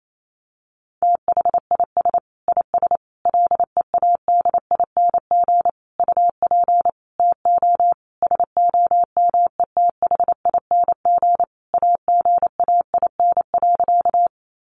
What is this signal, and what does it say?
Morse code, heard as a keyed tone, about 28 wpm, 700 Hz. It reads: THIS IS LEADING UP TO SOMETHING AGAIN.
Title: Sentences from Top 1100 Words 28wpm